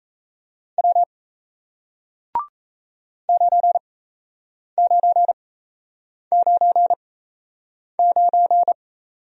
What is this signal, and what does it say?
Morse code, heard as a keyed tone, about 38 wpm, spaced as for 8 wpm, 700 Hz. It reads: W 9999